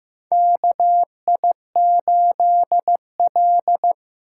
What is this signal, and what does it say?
Morse code, heard as a keyed tone, about 15 wpm, 700 Hz. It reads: KI8L
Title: USA Call Signs - 2x Character Spacing - 15wpm